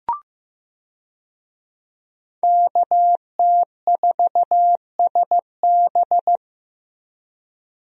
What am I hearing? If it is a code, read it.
Morse code, heard as a keyed tone, about 15 wpm, 700 Hz. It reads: KT4SB
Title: USA Call Signs - 5 Characters - Repeated 3x - 15wpm